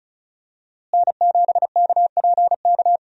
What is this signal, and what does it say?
Morse code, heard as a keyed tone, about 34 wpm, 700 Hz. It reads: N7KPK
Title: USA Call Signs - 5 Characters - Repeated 2x - 35wpm